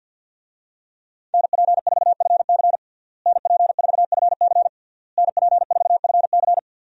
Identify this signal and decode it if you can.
Morse code, heard as a keyed tone, about 53 wpm, 700 Hz. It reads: NP4FC NP4FC NP4FC
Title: USA Call Signs - 5 Characters - Repeated 3x - 50wpm